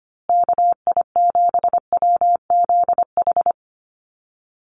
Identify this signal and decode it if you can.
Morse code, heard as a keyed tone, about 25 wpm, 700 Hz. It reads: KI7WZH
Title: USA Call Signs Repeated 2x 25wpm